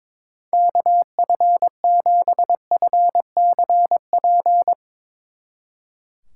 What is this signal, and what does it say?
Morse code, heard as a keyed tone, about 24 wpm, 700 Hz. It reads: KF7FCP
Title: Speed Racing - USA Call Signs 22wpm